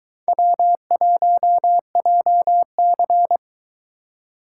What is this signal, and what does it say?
Morse code, heard as a keyed tone, about 22 wpm, 700 Hz. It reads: W1JC